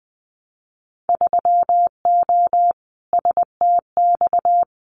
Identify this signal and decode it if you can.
Morse code, heard as a keyed tone, about 20 wpm, 700 Hz. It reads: 3O STX
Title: Winter Field Day Exchanges 20wpm